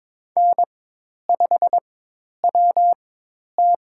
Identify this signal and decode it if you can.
Morse code, heard as a keyed tone, about 23 wpm, spaced as for 10 wpm, 700 Hz. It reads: N5WT
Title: USA Call Signs - 4 Characters - 4x Character Spacing - 22wpm